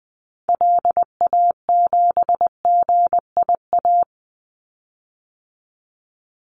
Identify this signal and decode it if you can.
Morse code, heard as a keyed tone, about 20 wpm, 700 Hz. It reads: LA7GIA
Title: Call Signs Repeated 2x - 20wpm